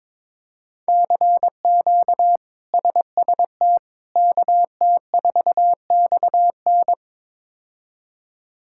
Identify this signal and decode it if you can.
Morse code, heard as a keyed tone, about 22 wpm, 700 Hz. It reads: CQ SST KT4XN
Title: K1USN SST QSO Practice - 22wpm